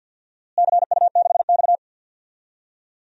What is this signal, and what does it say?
Morse code, heard as a keyed tone, about 50 wpm, 700 Hz. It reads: CU6X
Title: DXCC-100 Call Signs - 4 Characters - Repeated 2x - 50wpm